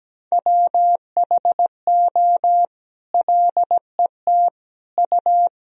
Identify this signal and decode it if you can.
Morse code, heard as a keyed tone, about 17 wpm, 700 Hz. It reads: WHO LET U
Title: Sentences from Top 300 Words and Abbreviations 17wpm